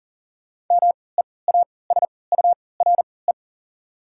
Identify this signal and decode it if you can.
Morse code, heard as a keyed tone, about 37 wpm, spaced as for 22 wpm, 700 Hz. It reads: MEASURE
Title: 3x Character Spacing - Top 500 Words - 40wpm